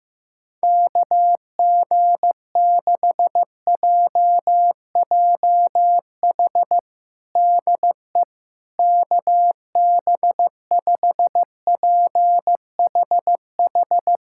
Morse code, heard as a keyed tone, about 15 wpm, 700 Hz. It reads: KG6JJH DE KB5PHH
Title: USA Callsign DE Callsign 15wpm